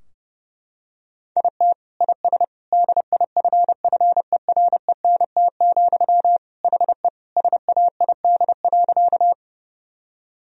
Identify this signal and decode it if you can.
Morse code, heard as a keyed tone, about 32 wpm, 700 Hz. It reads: IT IS DIFFERENT, HE SAID.
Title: Sentences from Top 1000 Words - Review Part 3 - 30wpm